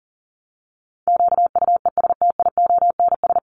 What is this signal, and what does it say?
Morse code, heard as a keyed tone, about 40 wpm, 700 Hz. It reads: QUESTIONS
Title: ES:S Suffix - Top 100 Words with Base Words 40wpm